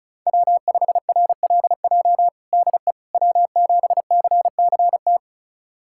Morse code, heard as a keyed tone, about 34 wpm, 700 Hz. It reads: W5RLJ DE W7CCT
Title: USA Callsign DE Callsign - 5 Characters - 35wpm